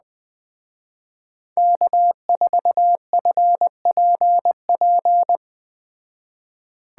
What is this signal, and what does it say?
Morse code, heard as a keyed tone, about 20 wpm, 700 Hz. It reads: K4FPP